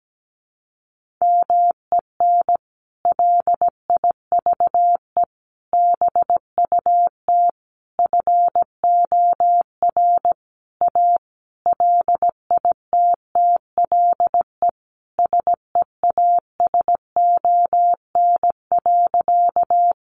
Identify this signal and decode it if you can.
Morse code, heard as a keyed tone, about 17 wpm, 700 Hz. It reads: MEN LIVE BUT FOR A LITTLE SEASON.
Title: Sentences from Top 1400 Words 17wpm